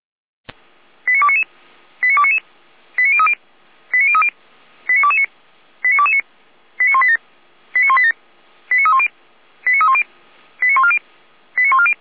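Five-tone selective call (ZVEI), heard as quick runs of five tones, tones of about 70 ms. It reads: 89299 89299 89030 89030 89229 89229 89187 89187 89310 89310 89140 89140
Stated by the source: Five-tone ZVEI selective call 89240, received on SDR, from the feuerwehr-orlamuende.org fire brigade audio server.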